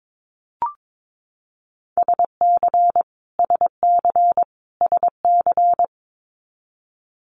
Morse code, heard as a keyed tone, about 22 wpm, 700 Hz. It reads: SC SC SC
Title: US States Repeated 3x 22wpm